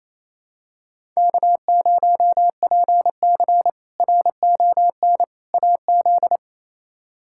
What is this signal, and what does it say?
Morse code, heard as a keyed tone, about 28 wpm, 700 Hz. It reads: K0PC RON AZ
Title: K1USN SST QSO Practice - 28wpm